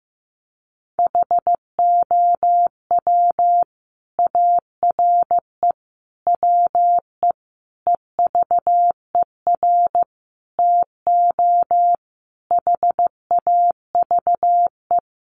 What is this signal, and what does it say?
Morse code, heard as a keyed tone, about 15 wpm, 700 Hz. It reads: HOW ARE WE EVER TO HAVE
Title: Sentences from Top 1500 Words 15wpm